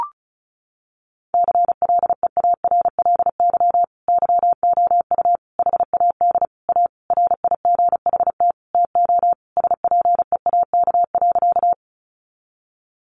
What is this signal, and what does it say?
Morse code, heard as a keyed tone, about 34 wpm, 700 Hz. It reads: CLEARLY YOU HAD A RIGHT TO SPEAK.
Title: Sentences from Top 1300 Words 35wpm